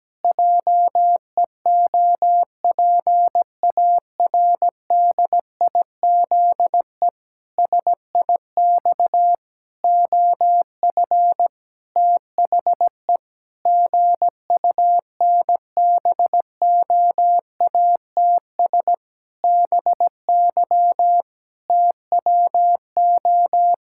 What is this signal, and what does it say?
Morse code, heard as a keyed tone, about 17 wpm, 700 Hz. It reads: JEOPARDIZE SIX OF THE GUNBOATS BY TWO
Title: Pangram Challenge v2 - 17wpm